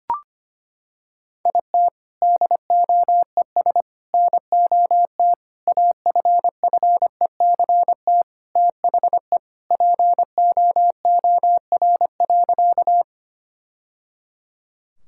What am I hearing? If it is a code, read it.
Morse code, heard as a keyed tone, about 25 wpm, 700 Hz. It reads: IT DOES NOT AFFECT THE POOR.
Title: Sentences from Top 1000 Words - Review Part 6 - 25wpm